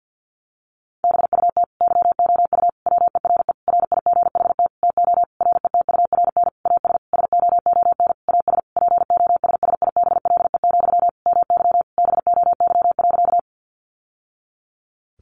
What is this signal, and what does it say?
Morse code, heard as a keyed tone, about 50 wpm, 700 Hz. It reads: BUT YOU WERE RIGHT TO RETURN AS SOON AS POSSIBLE, MY BOY.
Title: Sentences from Top 800 Words 50wpm